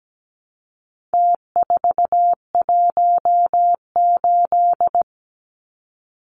Morse code, heard as a keyed tone, about 17 wpm, 700 Hz. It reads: T418
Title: Numbers - Contest Style - 4-Digit 17wpm